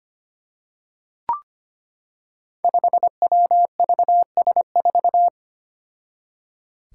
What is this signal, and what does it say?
Morse code, heard as a keyed tone, about 25 wpm, 700 Hz. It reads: 5WVS4
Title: Five Challenging Letter-Number Combinations 25wpm